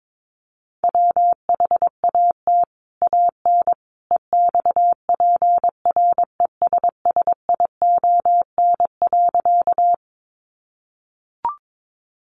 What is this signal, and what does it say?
Morse code, heard as a keyed tone, about 22 wpm, 700 Hz. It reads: WHAT AN EXPRESSION.
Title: Sentences from Top 1200 Words 22wpm